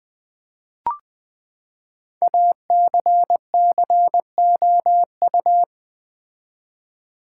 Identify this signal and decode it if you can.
Morse code, heard as a keyed tone, about 20 wpm, 700 Hz. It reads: ACCOU
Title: Build A Word - 5 Letters At A Time - 10-Letter Words - 20wpm